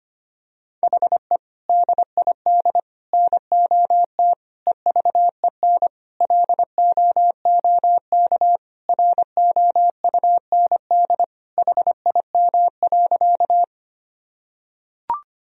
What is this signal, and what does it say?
Morse code, heard as a keyed tone, about 25 wpm, 700 Hz. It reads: HE DID NOT EVEN LOOK ROUND HIM.
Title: Sentences from Top 1000 Words - Review Part 1 - 25wpm